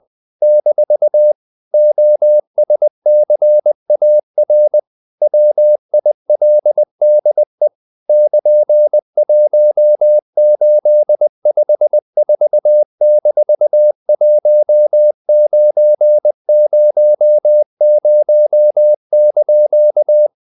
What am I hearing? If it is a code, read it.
Morse code, heard as a keyed tone, about 20 wpm, 600 Hz. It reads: - OSCAR WILDE <KN>1854-1900)
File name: CW / Practice / EnglishQuotations0320